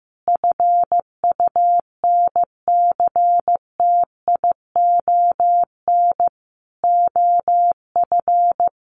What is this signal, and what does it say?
Morse code, heard as a keyed tone, about 15 wpm, 700 Hz. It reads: FUNCTION OF
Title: Sentences from Top 1100 Words 15wpm